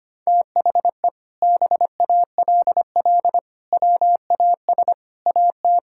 Morse code, heard as a keyed tone, about 25 wpm, 700 Hz. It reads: THE BALL WAS AT
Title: Sentences from Top 1600 Words 25wpm